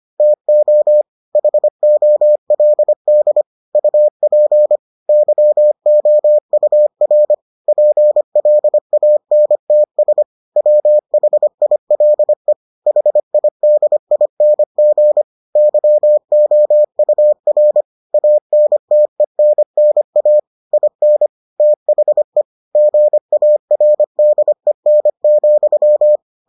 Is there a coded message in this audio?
Yes: Morse code from 0.2 to 26.2 s, heard as a keyed tone, about 25 wpm, 600 Hz. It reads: TO HOLD UP YOUR PLANTS WHILE HIDING YOUR ANTENNA IN THE GARDEN,